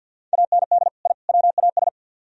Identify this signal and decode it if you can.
Morse code, heard as a keyed tone, about 62 wpm, 700 Hz. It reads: AND I WAS
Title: Speed-Racing - Sets of 3 Words 50wpm